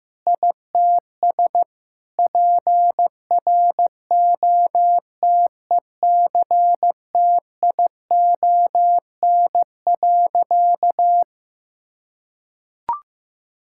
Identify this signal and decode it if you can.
Morse code, heard as a keyed tone, about 15 wpm, 700 Hz. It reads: ITS PROTECTION.
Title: Sentences from Top 1600 Words 15wpm